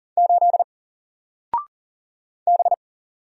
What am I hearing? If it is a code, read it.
Morse code, heard as a keyed tone, about 40 wpm, 700 Hz. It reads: 8 B
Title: Single Letter-Number Alt 40wpm